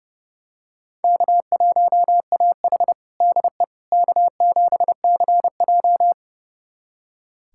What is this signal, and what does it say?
Morse code, heard as a keyed tone, about 32 wpm, 700 Hz. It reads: K1AH DE K7CJ
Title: USA Callsign DE Callsign - 4 Characters - 30wpm